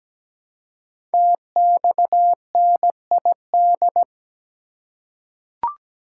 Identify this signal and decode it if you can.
Morse code, heard as a keyed tone, about 17 wpm, 700 Hz. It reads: TXNID
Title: Code Groups - 5 Letter:Numbers - 17wpm